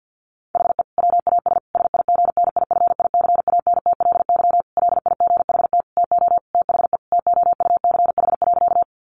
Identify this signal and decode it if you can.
Morse code, heard as a keyed tone, about 50 wpm, 700 Hz. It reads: HE WAS SIGNIFICANTLY LIGHT TO THE TOUCH.
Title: Sentences from Top 1500 Words 50wpm